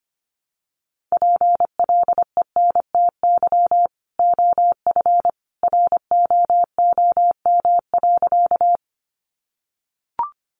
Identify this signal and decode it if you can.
Morse code, heard as a keyed tone, about 25 wpm, 700 Hz. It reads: PLENTY OF ROOM.
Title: Sentences from Top 1300 Words 25wpm